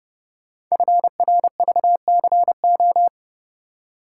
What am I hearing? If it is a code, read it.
Morse code, heard as a keyed tone, about 30 wpm, 700 Hz. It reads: FRVCO
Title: Code Groups - 5 Letters - 30wpm